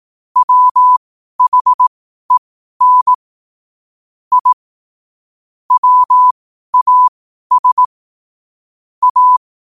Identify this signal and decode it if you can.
Morse code, heard as a keyed tone, about 18 wpm, spaced as for 12 wpm, 1000 Hz. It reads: WHEN I WAS A